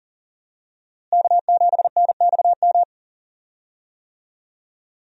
Morse code, heard as a keyed tone, about 40 wpm, 700 Hz. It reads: K7NXM
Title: USA Call Signs - 5 Characters - Repeated 2x - 40wpm